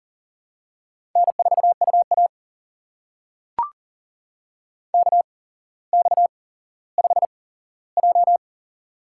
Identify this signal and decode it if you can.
Morse code, heard as a keyed tone, about 40 wpm, spaced as for 17 wpm, 700 Hz. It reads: N4UA KX5J
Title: USA Call Signs - 4 Characters - 8x Character Spacing - 40wpm